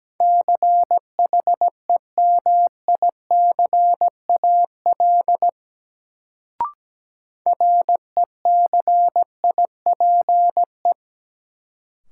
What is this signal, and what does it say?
Morse code, heard as a keyed tone, about 17 wpm, 700 Hz. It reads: CHEMICAL RECIPE
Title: Sentences from Top 1200 Words - New Word Drill Alt - 17wpm